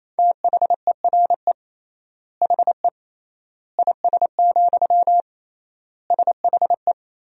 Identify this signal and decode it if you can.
Morse code, heard as a keyed tone, about 28 wpm, 700 Hz. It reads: THERE HE IS, SHE
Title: 3x Word-Spacing - Sentences from Top 100 Words - 28wpm